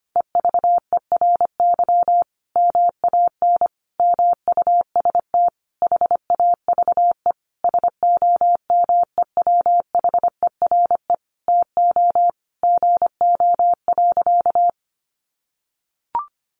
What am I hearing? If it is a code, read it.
Morse code, heard as a keyed tone, about 25 wpm, 700 Hz. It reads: EVERY MAN MUST HAVE SOMEWHERE TO GO.
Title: Sentences from Top 1400 Words 25wpm